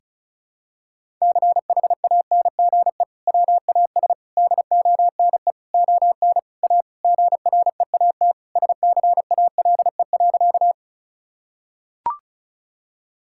Morse code, heard as a keyed tone, about 36 wpm, 700 Hz. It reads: CHANGE WAS DONE ON A GREAT SCALE.
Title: Sentences from Top 1000 Words - Review Part 5 - 35wpm